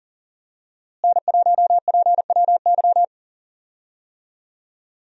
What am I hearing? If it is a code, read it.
Morse code, heard as a keyed tone, about 40 wpm, 700 Hz. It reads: N1PWY
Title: USA Call Signs - 5 Characters - Repeated 2x - 40wpm